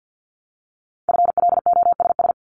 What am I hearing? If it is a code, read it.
Morse code, heard as a keyed tone, about 50 wpm, 700 Hz. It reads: FLOSS
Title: Multi-letter Phonemes Repeated 3x 50wpm